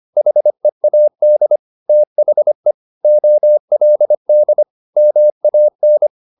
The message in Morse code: HEAD THE OLD MAN

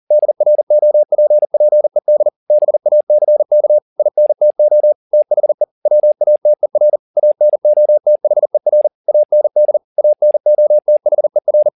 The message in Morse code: DROPPED BACK INTO THE WATER ANOTHER AND ANOTHER

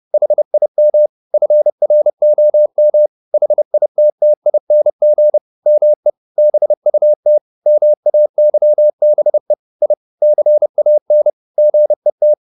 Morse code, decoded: HIM FROM HITTING ME BUT MAYBE I CAN GET

